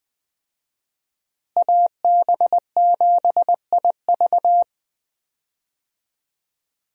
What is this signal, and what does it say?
Morse code, heard as a keyed tone, about 20 wpm, 700 Hz. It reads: AB7IV
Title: USA Call Signs - 5 Characters - Repeated 3x - 20wpm